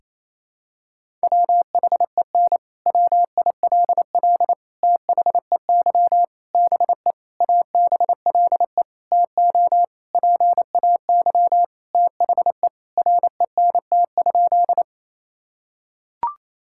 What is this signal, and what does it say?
Morse code, heard as a keyed tone, about 28 wpm, 700 Hz. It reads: WHEN WILL THEY BE ABLE TO PAY THE RENT?
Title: Sentences from Top 1000 Words - Review Part 5 - 28wpm